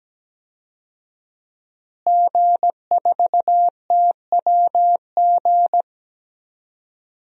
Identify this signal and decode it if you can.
Morse code, heard as a keyed tone, about 17 wpm, 700 Hz. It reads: G4TWG
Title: DXCC-100 Call Signs - 5 Characters - Repeated 2x - 17wpm